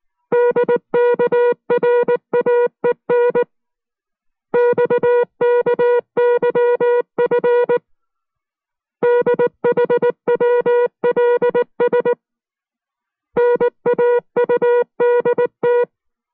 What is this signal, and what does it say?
Morse code, heard as a keyed tone, about 19 wpm, 480 Hz. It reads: DKRAE NXKYF DHWLS NAUDT